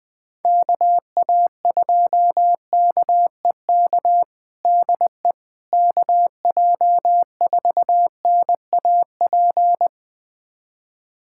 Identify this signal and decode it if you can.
Morse code, heard as a keyed tone, about 20 wpm, 700 Hz. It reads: KA2KEK DE KJ4NAP
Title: USA Callsign DE Callsign - 6 Characters - 20wpm